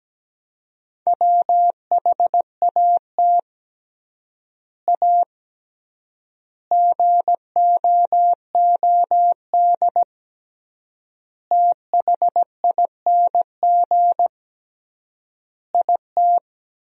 Morse code, heard as a keyed tone, about 17 wpm, 700 Hz. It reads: WHAT A GOOD THING IT
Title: 3x Word-Spacing - Sentences from Top 200 Words - 17wpm